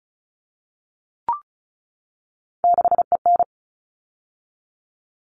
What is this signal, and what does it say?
Morse code, heard as a keyed tone, about 34 wpm, 700 Hz. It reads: 6EN
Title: All Cut Numbers - 3-Digit 35wpm